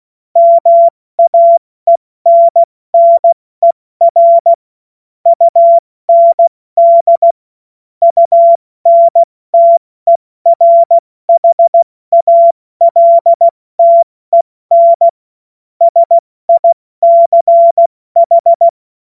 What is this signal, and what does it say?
Morse code, heard as a keyed tone, about 16 wpm, 680 Hz. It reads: MAENNER UND UNTERHALTEN SICH
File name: Morsekurs / Witze / Witz 01. Februar 2022 70 F80